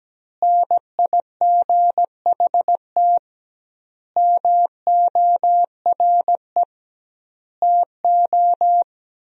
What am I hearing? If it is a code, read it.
Morse code, heard as a keyed tone, about 17 wpm, 700 Hz. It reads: NIGHT MORE TO
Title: 2x Word-Spacing - Sentences from Top 300 Words - 17wpm